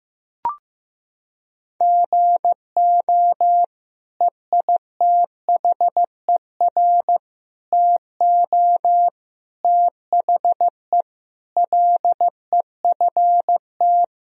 Morse code, heard as a keyed tone, about 15 wpm, 700 Hz. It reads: GO EITHER TO THE LEFT
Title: Sentences from Top 1000 Words - Review Part 4 - 15wpm